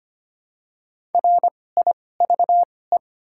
Morse code, heard as a keyed tone, about 25 wpm, spaced as for 18 wpm, 700 Hz. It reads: RIVE
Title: 2x Character Spacing - Top 300 Words - 25wpm